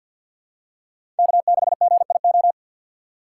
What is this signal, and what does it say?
Morse code, heard as a keyed tone, about 47 wpm, 700 Hz. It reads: K6GIO
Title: USA Call Signs - 5 Characters - 3x Character Spacing - 50wpm